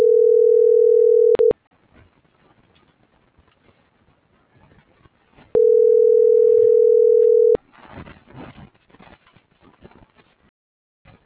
Ringback tone (North America), 440 + 480 Hz.